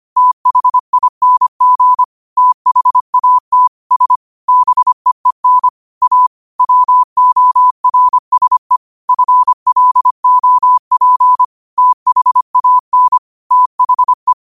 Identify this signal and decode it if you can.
Morse code, heard as a keyed tone, about 25 wpm, 1000 Hz. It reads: THING THAT S BEEN A WORSE FLOP THAN THE
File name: qotdfiles / 25wpm